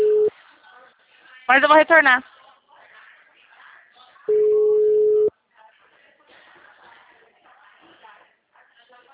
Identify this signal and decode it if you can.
Ringback tone (Europe, 425 Hz), with speech.
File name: out- -2008- - - .73356